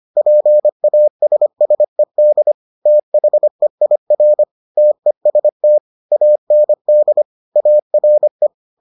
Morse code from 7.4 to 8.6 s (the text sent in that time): ARE